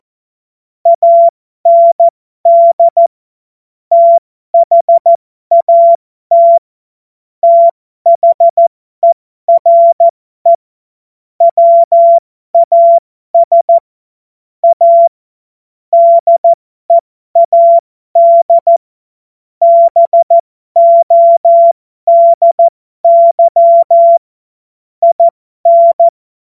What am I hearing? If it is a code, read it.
Morse code, heard as a keyed tone, about 14 wpm, spaced as for 12 wpm, 680 Hz. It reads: AND THAT THERE WAS A DEAD BODY IN